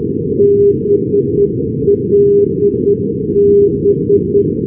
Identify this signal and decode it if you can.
Morse code, heard as a keyed tone, about 10 wpm, 400 Hz, with noise. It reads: BLB
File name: ndb blois-le breuil blb(e)-397